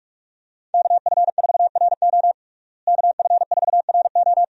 Morse code, heard as a keyed tone, about 42 wpm, 700 Hz. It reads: KF4RO KF4RO
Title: USA Call Signs - 5 Characters - Repeated 2x - 45wpm